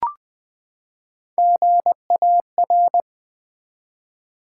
Morse code, heard as a keyed tone, about 20 wpm, 700 Hz. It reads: GAR
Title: Build A Word - 3 Letters At A Time - 6-Letter Words - 20wpm